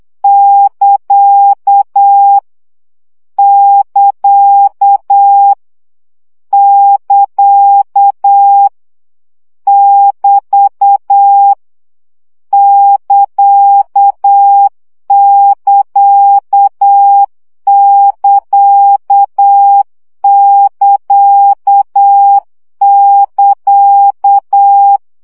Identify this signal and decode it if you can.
Morse code, heard as a keyed tone, about 8 wpm, 800 Hz. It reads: <KA> <KA> <KA> <BT> <KA><KA><KA><KA><KA>
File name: CW gyakorlas ct 7 baud